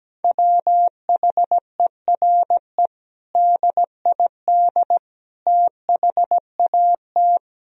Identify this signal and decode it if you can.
Morse code, heard as a keyed tone, about 17 wpm, 700 Hz. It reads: WHERE DID THAT
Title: Sentences from Top 1000 Words - Review Part 1 - 17wpm